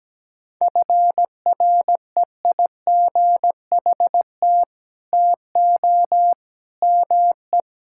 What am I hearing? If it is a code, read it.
Morse code, heard as a keyed tone, about 17 wpm, 700 Hz. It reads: FREIGHT TO ME
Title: Pangram Challenge v2 - 17wpm